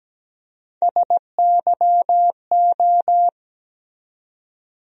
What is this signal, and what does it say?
Morse code, heard as a keyed tone, about 17 wpm, 700 Hz. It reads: SYO
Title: USA Call Sign Postfixes - 3 Letters 17wpm